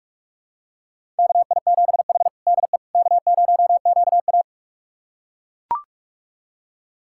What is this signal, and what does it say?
Morse code, heard as a keyed tone, about 42 wpm, 700 Hz. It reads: KI7H DE K0QA
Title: USA Callsign DE Callsign - 4 Characters - 45wpm